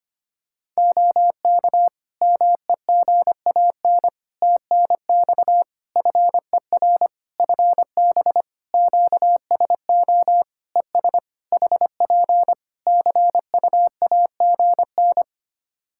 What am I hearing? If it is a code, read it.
Morse code, heard as a keyed tone, about 25 wpm, 700 Hz. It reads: OK MEGAN TNX FER FB QSO ES HP CUAGN
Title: Practice QSO - Exchange 4 - Ending - 25wpm